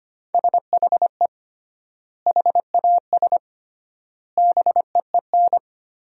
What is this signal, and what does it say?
Morse code, heard as a keyed tone, about 25 wpm, 700 Hz. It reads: SHE HAS BEEN
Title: 3x Word-Spacing - Sentences from Top 200 Words - 25wpm